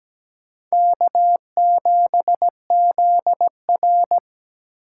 Morse code, heard as a keyed tone, about 20 wpm, 700 Hz. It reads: K7ZR
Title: Speed Racing - USA 1x2 Call Signs 17wpm